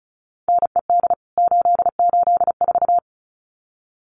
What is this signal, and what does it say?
Morse code, heard as a keyed tone, about 34 wpm, 700 Hz. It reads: NED 884
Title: CWT Exchange 35wpm